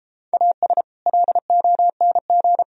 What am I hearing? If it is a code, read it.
Morse code, heard as a keyed tone, about 29 wpm, 700 Hz. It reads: AS LONG